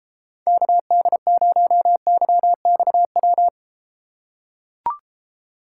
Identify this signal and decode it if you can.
Morse code, heard as a keyed tone, about 33 wpm, 700 Hz. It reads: KD0YXW